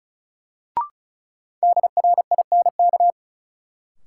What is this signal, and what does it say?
Morse code, heard as a keyed tone, about 34 wpm, 700 Hz. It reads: DRINK
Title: Sentences from Top 1500 Words - New Word Drill - 35wpm